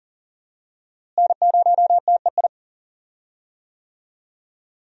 Morse code, heard as a keyed tone, about 40 wpm, 700 Hz. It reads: N0TEI
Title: USA Call Signs - 5 Characters - Repeated 3x - 40wpm